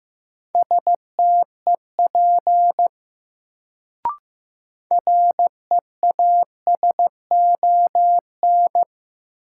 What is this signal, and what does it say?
Morse code, heard as a keyed tone, about 15 wpm, 700 Hz. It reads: STEP REASON
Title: Sets of 5 Words Encore - New Word Drill - 15wpm